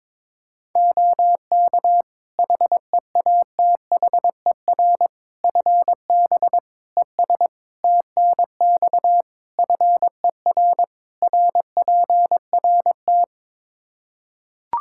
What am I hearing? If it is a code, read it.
Morse code, heard as a keyed tone, about 22 wpm, 700 Hz. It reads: OK HEATHER FB ES TNX FER RPRT
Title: K1USN SST QSO Practice 22wpm